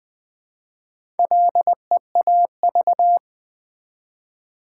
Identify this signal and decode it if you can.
Morse code, heard as a keyed tone, about 20 wpm, 700 Hz. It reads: LEAV
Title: Build A Word - 1 Letter At A Time - 3 to 5-Letter Words - 20wpm